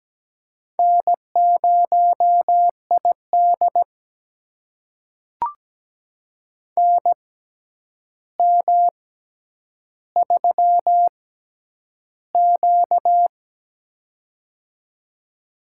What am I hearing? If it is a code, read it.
Morse code, heard as a keyed tone, about 17 wpm, spaced as for 9 wpm, 700 Hz. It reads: N0ID NM3Q
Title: USA Call Signs - 4 Characters - 6x Character Spacing - 17wpm